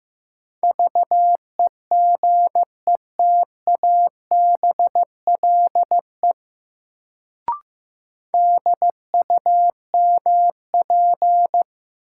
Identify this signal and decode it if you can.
Morse code, heard as a keyed tone, about 15 wpm, 700 Hz. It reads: VEGETABLE DUMP
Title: Sentences from Top 1300 Words - New Word Drill - 15wpm